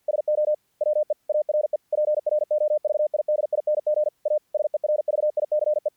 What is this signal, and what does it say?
Morse code, heard as a keyed tone, about 48 wpm, 600 Hz. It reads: SO WE ARE PROVIDING A SERVICE
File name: archived / CRSnet